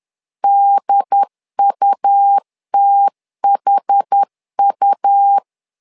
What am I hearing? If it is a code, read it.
Morse code, heard as a keyed tone, about 10 wpm, 800 Hz. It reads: DUTHU
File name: LEZIONI / 2 UHDN / 2 UHD 10wpm 1826